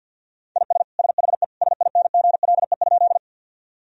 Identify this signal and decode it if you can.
Morse code, heard as a keyed tone, about 50 wpm, 700 Hz. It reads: IS SHE SINGLE?
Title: Sentences from Top 1000 Words - Review Part 4 - 50wpm